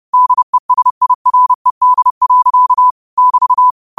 Morse code, heard as a keyed tone, about 30 wpm, 1000 Hz. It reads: DESIRED. <BT>
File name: qotdfiles / 30wpm